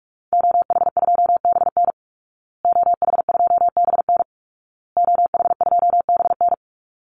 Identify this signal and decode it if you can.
Morse code, heard as a keyed tone, about 44 wpm, 700 Hz. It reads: OH2BN OH2BN OH2BN
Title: Call Signs Repeated 2x - 45wpm